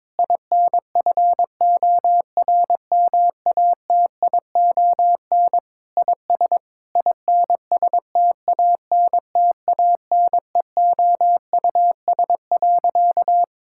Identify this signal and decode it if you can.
Morse code, heard as a keyed tone, about 22 wpm, 700 Hz. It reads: INFORMATION IS INSTANTANEOUS.
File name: Difficult Sentences with Hints - 22wpm